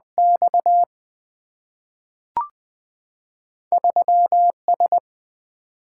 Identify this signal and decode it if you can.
Morse code, heard as a keyed tone, about 20 wpm, 700 Hz. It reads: X 3 S
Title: Two Letter-Number Combinations - 20wpm